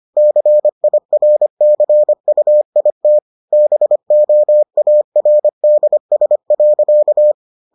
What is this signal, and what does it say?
Morse code, heard as a keyed tone, about 25 wpm, 600 Hz. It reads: CIRCUIT BOARDS.